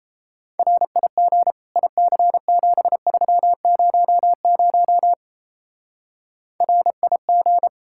Morse code, heard as a keyed tone, about 35 wpm, 700 Hz. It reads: RIG IC7300 RIG
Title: Speed-Racing Common QSO Phrases - 28wpm